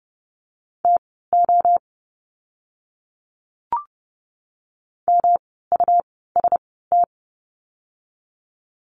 Morse code, heard as a keyed tone, about 28 wpm, spaced as for 17 wpm, 700 Hz. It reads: TO MUST
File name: course / 30:10wpm / 237 Lesson 38f - Letter q - Words 30wpm